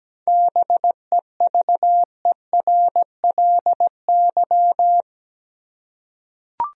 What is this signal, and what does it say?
Morse code, heard as a keyed tone, about 17 wpm, 700 Hz. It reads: BEVERLY
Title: USA Common Names - 17wpm